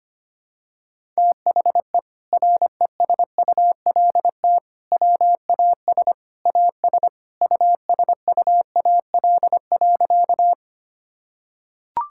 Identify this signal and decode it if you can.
Morse code, heard as a keyed tone, about 25 wpm, 700 Hz. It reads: THE RESULT WAS AS USUAL.
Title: Sentences from Top 1000 Words - Review Part 5 - 25wpm